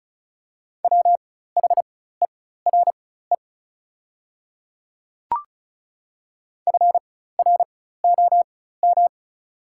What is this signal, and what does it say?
Morse code, heard as a keyed tone, about 36 wpm, spaced as for 16 wpm, 690 Hz. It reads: WHERE FROM
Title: 4x Character Spacing - Top 300 Words - 35wpm